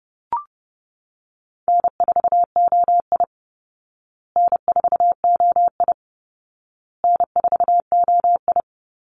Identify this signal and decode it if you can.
Morse code, heard as a keyed tone, about 30 wpm, 700 Hz. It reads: N4OI N4OI N4OI
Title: Easy Call Signs Repeated 3x - 30wpm